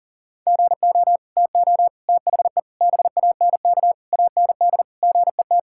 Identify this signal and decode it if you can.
Morse code, heard as a keyed tone, about 37 wpm, 700 Hz. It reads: GO TO THE BANK AND GET
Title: Sentences from Top 1000 Words - Review Part 5 - 40wpm